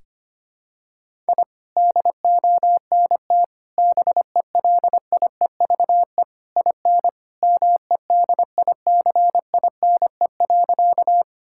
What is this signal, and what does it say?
Morse code, heard as a keyed tone, about 25 wpm, 700 Hz. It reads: I DONT BELIEVE IN MEDICINE.
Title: Sentences from Top 1000 Words - Review Part 6 - 25wpm